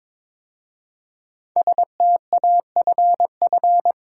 Morse code, heard as a keyed tone, about 22 wpm, 700 Hz. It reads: STAFF
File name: 5 Letter Words - First 3 Letters Given - 22wpm